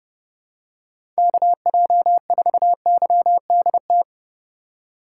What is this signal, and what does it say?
Morse code, heard as a keyed tone, about 29 wpm, 700 Hz. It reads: KJ4YDT